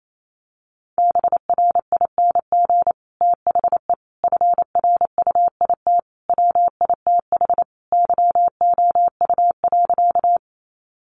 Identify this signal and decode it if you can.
Morse code, heard as a keyed tone, about 28 wpm, 700 Hz. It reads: BRING THE FRUIT WITH YOU.
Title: Sentences from Top 1300 Words 28wpm